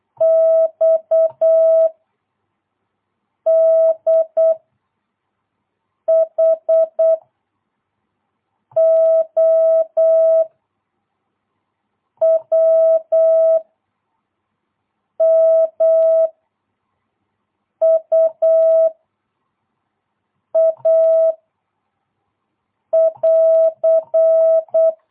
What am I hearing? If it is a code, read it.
Morse code, heard as a keyed tone, about 8 wpm, 660 Hz. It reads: X D H O W M U A <AR>
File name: jh8cbh / 2022cw